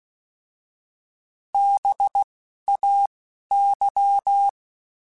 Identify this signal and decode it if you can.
Morse code, heard as a keyed tone, about 16 wpm, spaced as for 11 wpm, 800 Hz. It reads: BAY
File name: Alphabetical / Alphabetical mp3s / Bs @10